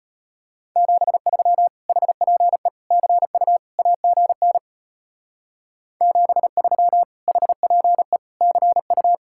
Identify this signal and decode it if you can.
Morse code, heard as a keyed tone, about 31 wpm, 700 Hz. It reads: 73 HPE CU AGN 73 HPE CU
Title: Speed-Racing Common QSO Phrases - 25wpm